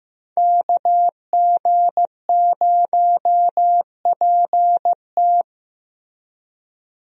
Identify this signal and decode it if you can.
Morse code, heard as a keyed tone, about 15 wpm, 700 Hz. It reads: KG0PT
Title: USA Call Signs - 5 Characters - Repeated 3x - 15wpm